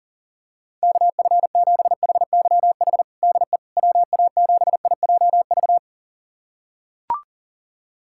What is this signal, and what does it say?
Morse code, heard as a keyed tone, about 40 wpm, 700 Hz. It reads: KF7HYH DE WA7IJV
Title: USA Callsign DE Callsign - 6 Characters - 40wpm